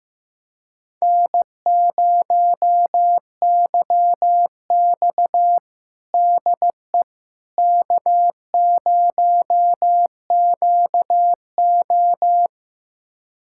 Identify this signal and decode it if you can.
Morse code, heard as a keyed tone, about 15 wpm, 700 Hz. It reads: N0YX DE K0QO
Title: USA Callsign DE Callsign - 4 Characters - 15wpm